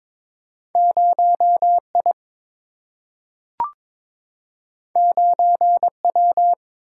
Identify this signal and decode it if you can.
Morse code, heard as a keyed tone, about 22 wpm, 700 Hz. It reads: 0I 9W
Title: Single-Letter Single-Number 22wpm